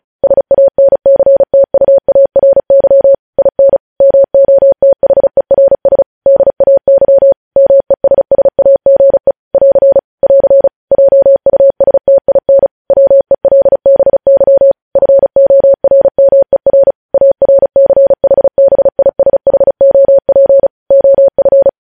Morse code, heard as a keyed tone, about 36 wpm, 550 Hz. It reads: SANCTUARY IN MOTHERS DAY MESSAGE <AR> <AR> JUSTIN WELBY FORMER ARCHBISHOP OF
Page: News headlines in Morse code at 35 wpm